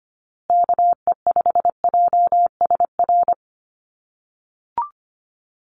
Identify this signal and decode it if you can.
Morse code, heard as a keyed tone, about 25 wpm, 700 Hz. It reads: KE5JSR
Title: USA Call Signs Repeated 3x 25wpm